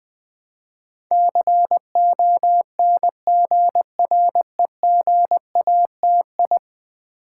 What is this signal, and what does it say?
Morse code, heard as a keyed tone, about 20 wpm, 700 Hz. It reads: CONGREGATI
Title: Build A Word - 5 Letters At A Time - 12-Letter Words - 20wpm